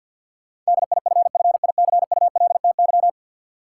Morse code, heard as a keyed tone, about 53 wpm, 700 Hz. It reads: DIFFICULTY
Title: Sentences from Top 1500 Words - New Word Drill Alt - 50wpm